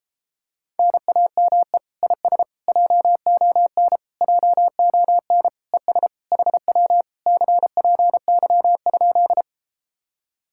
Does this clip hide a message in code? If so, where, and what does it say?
Yes: Morse code from 0.8 to 9.4 s, heard as a keyed tone, about 33 wpm, 700 Hz. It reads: NAME IS JON JON ES HW CPY?